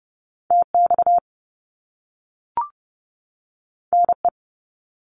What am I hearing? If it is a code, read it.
Morse code, heard as a keyed tone, about 30 wpm, 700 Hz. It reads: TX NE
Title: US States 30wpm